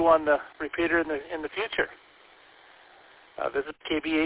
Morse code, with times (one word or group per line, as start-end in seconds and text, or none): none